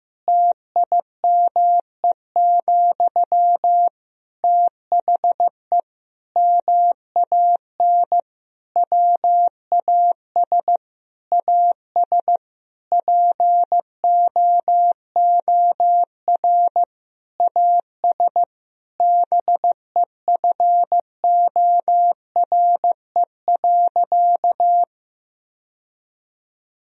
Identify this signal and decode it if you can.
Morse code, heard as a keyed tone, about 16 wpm, 700 Hz. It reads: TIME, THE MAN WAS AS POOR AS BEFORE.
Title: Sentences from Top 1000 Words - Review Part 2 - 15wpm